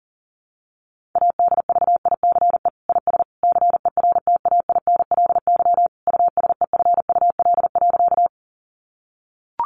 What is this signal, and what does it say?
Morse code, heard as a keyed tone, about 40 wpm, 700 Hz. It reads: ADVICE IS CERTAINLY USEFUL.
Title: Sentences from Top 1100 Words 40wpm